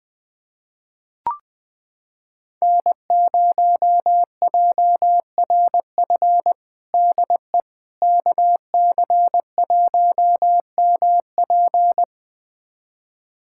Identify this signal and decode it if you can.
Morse code, heard as a keyed tone, about 20 wpm, 700 Hz. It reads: N0JRF DE KC1MP
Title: USA Callsign DE Callsign - 5 Characters - 20wpm